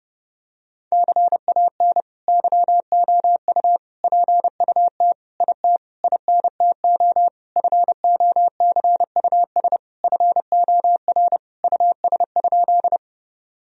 Morse code, heard as a keyed tone, about 32 wpm, 700 Hz. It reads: CAN YOU PUT IT INTO FOCUS FOR US?
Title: Sentences from Top 1000 Words - Review Part 4 - 30wpm